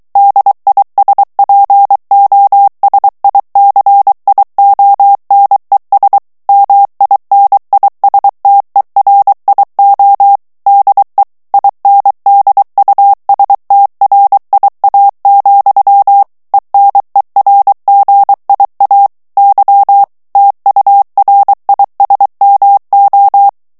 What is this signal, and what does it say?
Morse code, heard as a keyed tone, about 23 wpm, 800 Hz. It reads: DISPOSICIONES MINISTERIO DE INDUSTRIA, ENERGIA Y TURISMO
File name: ESPAGNOL TEXTE EN CLAIR 2 VITESSE 22 MOTS MINUTES